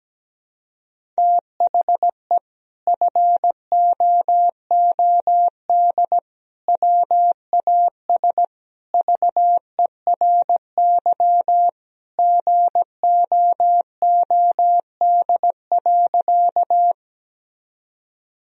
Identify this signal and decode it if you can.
Morse code, heard as a keyed tone, about 17 wpm, 700 Hz. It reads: THE FOOD WAS VERY GOOD.
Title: Sentences from Top 1000 Words - Review Part 1 - 17wpm